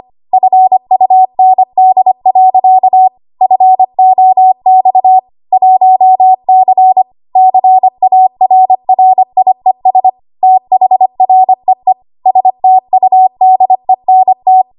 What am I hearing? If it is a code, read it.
Morse code, heard as a keyed tone, about 25 wpm, 750 Hz. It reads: FUND. FOX 1C CARRIES THREE STUDENT
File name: Archive / 25 WPM / 25WPM